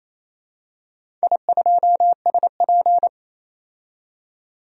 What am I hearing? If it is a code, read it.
Morse code, heard as a keyed tone, about 28 wpm, 700 Hz. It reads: I2SP
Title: DXCC-100 Call Signs - 4 Characters - 28wpm